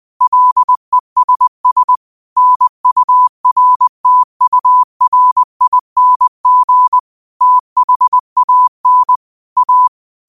Morse code, heard as a keyed tone, about 20 wpm, 1000 Hz. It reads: LESS NURTURING THAN A
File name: qotdfiles / 20wpm